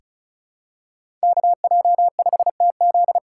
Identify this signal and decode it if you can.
Morse code, heard as a keyed tone, about 36 wpm, 700 Hz. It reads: KJ5TZ
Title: USA Call Signs - 5 Characters - 3x Character Spacing - 35wpm